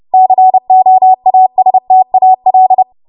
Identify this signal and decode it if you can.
Morse code, heard as a keyed tone, about 30 wpm, 750 Hz. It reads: COASTAL
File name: Morse / 30WPM